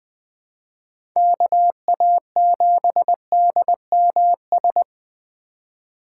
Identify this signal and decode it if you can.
Morse code, heard as a keyed tone, about 21 wpm, 700 Hz. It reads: KA7DMS
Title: USA Call Signs - 6 Characters - 4x Character Spacing - 20wpm